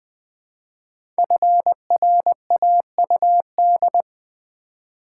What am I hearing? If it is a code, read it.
Morse code, heard as a keyed tone, about 20 wpm, 700 Hz. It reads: FRAUD
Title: Build A Word - 1 Letter At A Time - 3 to 5-Letter Words - 20wpm